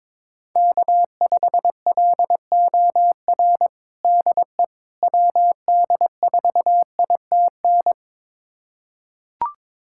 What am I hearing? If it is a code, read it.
Morse code, heard as a keyed tone, about 22 wpm, 700 Hz. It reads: K5LOR DE WD4ITN
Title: USA Callsign DE Callsign 22wpm